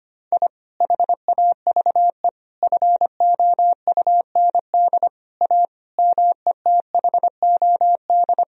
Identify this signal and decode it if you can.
Morse code, heard as a keyed tone, about 25 wpm, 700 Hz. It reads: I HAVE FOUND A METHOD
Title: Sentences from Top 1000 Words - Review Part 3 - 25wpm